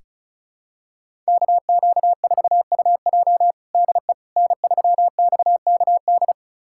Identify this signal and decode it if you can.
Morse code, heard as a keyed tone, about 36 wpm, 700 Hz. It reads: KQ4UJ DE N3XKD
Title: USA Callsign DE Callsign - 5 Characters - 35wpm